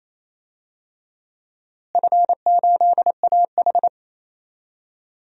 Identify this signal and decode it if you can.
Morse code, heard as a keyed tone, about 28 wpm, 700 Hz. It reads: F8AH
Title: DXCC-100 Call Signs - 4 Characters - Repeated 3x - 28wpm